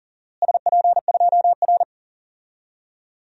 Morse code, heard as a keyed tone, about 39 wpm, 700 Hz. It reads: SP2R
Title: DXCC-100 Call Signs - 4 Characters - Repeated 2x - 40wpm